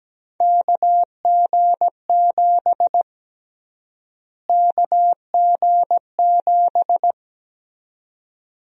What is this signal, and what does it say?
Morse code, heard as a keyed tone, about 17 wpm, 700 Hz. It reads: KG7 KG7
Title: USA Call Sign Prefixes - Repeated 2x - 17wpm